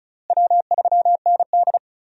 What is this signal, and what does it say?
Morse code, heard as a keyed tone, about 36 wpm, 700 Hz. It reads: W3ND
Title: USA Call Signs - 4 Characters - Repeated 3x - 35wpm